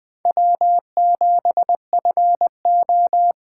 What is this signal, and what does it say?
Morse code, heard as a keyed tone, about 21 wpm, 700 Hz. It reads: W7FO